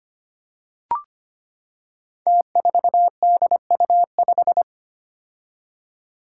Morse code, heard as a keyed tone, about 25 wpm, 700 Hz. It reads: T4DU5
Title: Five Challenging Letter-Number Combinations 25wpm